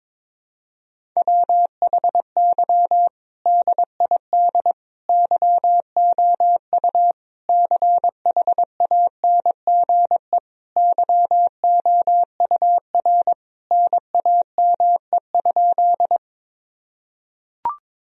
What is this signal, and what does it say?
Morse code, heard as a keyed tone, about 22 wpm, 700 Hz. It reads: WHY DID YOU CHANGE YOUR NAME?
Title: Sentences from Top 1000 Words - Review Part 1 - 22wpm